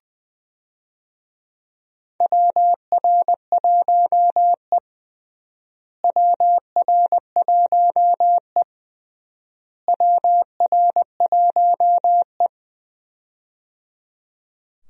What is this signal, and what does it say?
Morse code, heard as a keyed tone, about 20 wpm, 700 Hz. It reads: WR1E WR1E WR1E
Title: USA Call Signs - 4 Characters - Repeated 3x - 20wpm